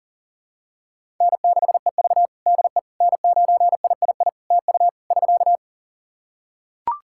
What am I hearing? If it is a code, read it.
Morse code, heard as a keyed tone, about 37 wpm, 700 Hz. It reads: N6EV DE N9III TU <SK>
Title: Practice QSO - Exchange 4 - Ending - 40wpm